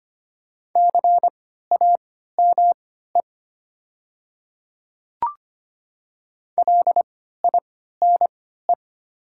Morse code, heard as a keyed tone, about 25 wpm, spaced as for 14 wpm, 700 Hz. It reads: CAME LINE
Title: 3x Character Spacing - Top 300 Words - 25wpm